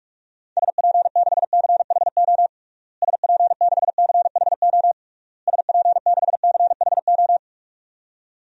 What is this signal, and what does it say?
Morse code, heard as a keyed tone, about 46 wpm, 700 Hz. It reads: SP6CHO SP6CHO SP6CHO
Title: DXCC-100 Call Signs - 6 Characters - Repeated 3x - 45wpm